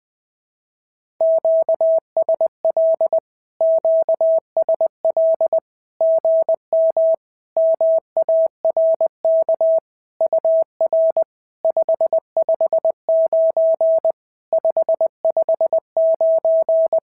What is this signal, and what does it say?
Morse code, heard as a keyed tone, about 20 wpm, 650 Hz. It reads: QSL QSL GM MARK UR 559 559
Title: SOTA QSO Practice 20wpm